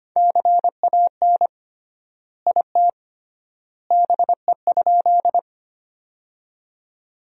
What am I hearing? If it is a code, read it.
Morse code, heard as a keyed tone, about 25 wpm, 700 Hz. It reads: CAN IT BE?
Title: 3x Word-Spacing - Sentences from Top 100 Words - 25wpm